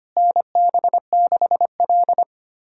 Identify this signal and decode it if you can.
Morse code, heard as a keyed tone, about 25 wpm, 700 Hz. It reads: NB6L
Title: USA Callsign DE Callsign - 4 Characters - 25wpm